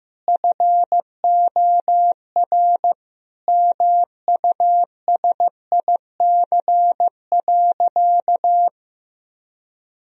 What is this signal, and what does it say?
Morse code, heard as a keyed tone, about 15 wpm, 700 Hz. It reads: FOR MUSIC.
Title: Sentences from Top 1000 Words - Review Part 1 - 15wpm